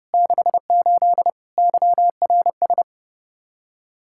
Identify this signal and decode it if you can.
Morse code, heard as a keyed tone, about 38 wpm, 700 Hz. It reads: 68 YRS